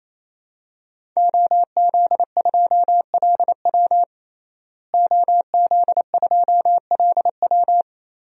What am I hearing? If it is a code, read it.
Morse code, heard as a keyed tone, about 27 wpm, 700 Hz. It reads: OZ2LW OZ2LW
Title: DXCC-100 Call Signs - 5 Characters - Repeated 3x - 28wpm